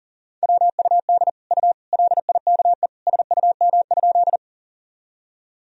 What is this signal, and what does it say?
Morse code, heard as a keyed tone, about 40 wpm, 700 Hz. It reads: WUD U LIKE SUM?
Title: Sentences from Top 100 Words and Abbreviations 40wpm